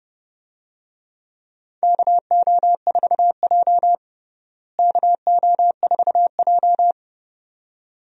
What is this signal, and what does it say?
Morse code, heard as a keyed tone, about 30 wpm, 700 Hz. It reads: KO4J KO4J
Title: USA Call Signs - 4 Characters - Repeated 2x - 30wpm